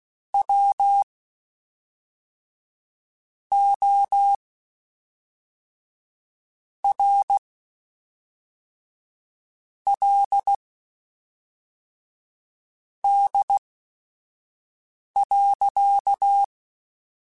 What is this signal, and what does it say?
Morse code, heard as a keyed tone, about 16 wpm, spaced as for 4 wpm, 800 Hz. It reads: WORLD.
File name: Quotes Spaced / Quotes Spaced-mp3s / ChildAsleep sp @5